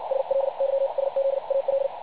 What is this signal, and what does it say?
Morse code, heard as a keyed tone, about 58 wpm, spaced as for 47 wpm, 550 Hz, with radio noise. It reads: SLYSZAL
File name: pub / record / websdr recording 3564.8kHz